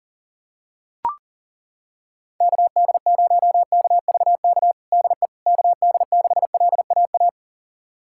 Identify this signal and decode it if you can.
Morse code, heard as a keyed tone, about 40 wpm, 700 Hz. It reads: KD0KVK DE KD6LAA
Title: USA Callsign DE Callsign - 6 Characters - 40wpm